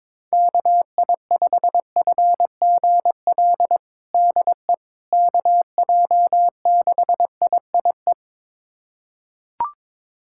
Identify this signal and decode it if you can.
Morse code, heard as a keyed tone, about 22 wpm, 700 Hz. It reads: KI5FGL DE KJ6IIE
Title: USA Callsign DE Callsign - 6 Characters - 22wpm